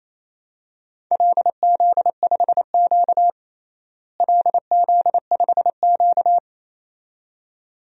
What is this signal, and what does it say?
Morse code, heard as a keyed tone, about 28 wpm, 700 Hz. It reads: LZ5Q LZ5Q
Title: DXCC-100 Call Signs - 4 Characters - Repeated 2x - 28wpm